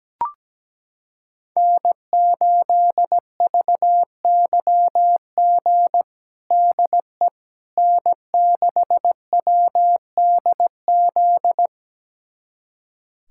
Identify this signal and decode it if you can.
Morse code, heard as a keyed tone, about 17 wpm, 700 Hz. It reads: N8VYG DE N6WDZ
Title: USA Callsign DE Callsign - 5 Characters - 17wpm